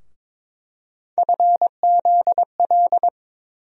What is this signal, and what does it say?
Morse code, heard as a keyed tone, about 22 wpm, 700 Hz. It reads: FZL
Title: USA Call Sign Postfixes - 3 Letters 22wpm